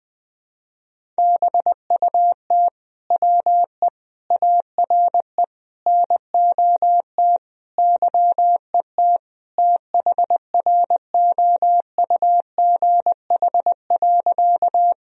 Morse code, heard as a keyed tone, about 20 wpm, 700 Hz. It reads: BUT WE ARE NOT YET THROUGH.
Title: Sentences from Top 1000 Words - Review Part 1 - 20wpm